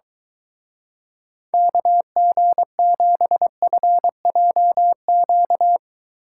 Morse code, heard as a keyed tone, about 22 wpm, 700 Hz. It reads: KG7FJQ